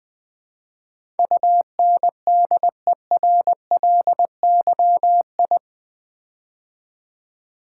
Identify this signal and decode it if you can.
Morse code, heard as a keyed tone, about 20 wpm, 700 Hz. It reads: UNDERLYI
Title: Build A Word - 4 Letters At A Time - 10-Letter Words - 20wpm